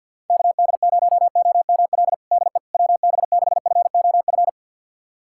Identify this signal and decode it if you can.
Morse code, heard as a keyed tone, about 50 wpm, 700 Hz. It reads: KD0OML DE WB6FOF
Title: USA Callsign DE Callsign 50wpm